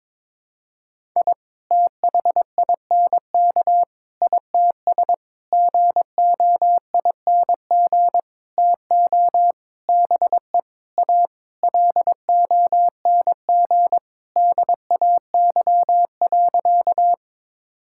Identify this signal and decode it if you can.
Morse code, heard as a keyed tone, about 22 wpm, 700 Hz. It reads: I THINK ITS GOING TO BE A LONG DAY.
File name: Sentences with 7-Word Phrases - 22wpm